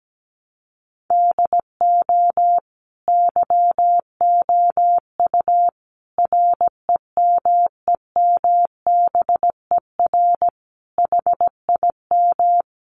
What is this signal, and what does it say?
Morse code, heard as a keyed tone, about 17 wpm, 700 Hz. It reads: DO YOU REMEMBER HIM
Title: Sentences from Top 1300 Words 17wpm